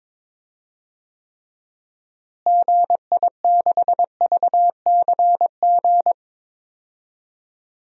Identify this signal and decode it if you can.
Morse code, heard as a keyed tone, about 22 wpm, 700 Hz. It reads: GI6VCG
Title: DXCC-100 Call Signs - 6 Characters - Repeated 2x - 22wpm